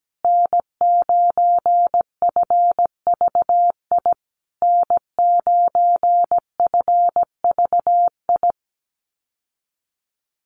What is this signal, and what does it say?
Morse code, heard as a keyed tone, about 17 wpm, 700 Hz. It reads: N9FVI N9FVI
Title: USA Call Signs Repeated 3x 17wpm